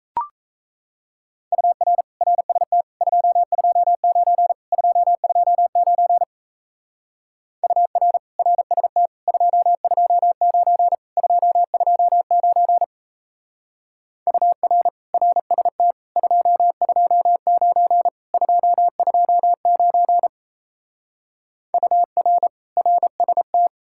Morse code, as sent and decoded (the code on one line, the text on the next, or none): ..- .-. / .-. ... - / ..--- ..--- ----. / ..--- ..--- ----. / ..- .-. / .-. ... - / ..--- ..--- ----. / ..--- ..--- ----. / ..- .-. / .-. ... - / ..--- ..--- ----. / ..--- ..--- ----. / ..- .-. / .-. ... -
UR RST 229 229 UR RST 229 229 UR RST 229 229 UR RST